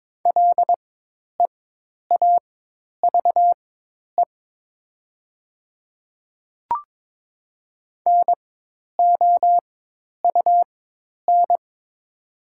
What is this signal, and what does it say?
Morse code, heard as a keyed tone, about 22 wpm, spaced as for 10 wpm, 700 Hz. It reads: LEAVE NOUN
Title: 4x Character Spacing - Top 500 Words - 22wpm